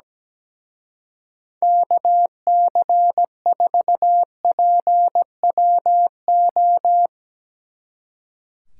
Morse code, heard as a keyed tone, about 17 wpm, 700 Hz. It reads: KC4PWO
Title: Speed Racing - USA 2x3 Call Signs 17wpm